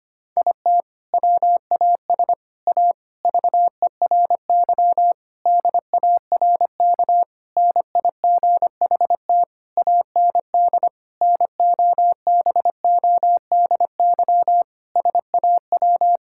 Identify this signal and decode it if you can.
Morse code, heard as a keyed tone, about 25 wpm, 700 Hz. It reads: IT WAS A VERY DARK NIGHT AND NOBODY SAW
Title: Sentences from Top 1600 Words 25wpm